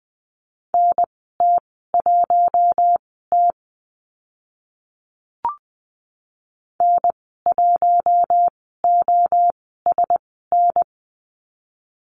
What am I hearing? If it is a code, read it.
Morse code, heard as a keyed tone, about 20 wpm, spaced as for 14 wpm, 700 Hz. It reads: NT1T N1OSN
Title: 042 Lesson 8g - Number 1 - Call Signs 20wpm